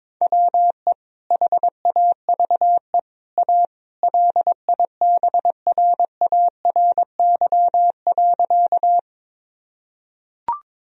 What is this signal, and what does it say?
Morse code, heard as a keyed tone, about 22 wpm, 700 Hz. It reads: WE HAVE A LIBRARY.
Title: Sentences from Top 1000 Words - Review Part 4 - 22wpm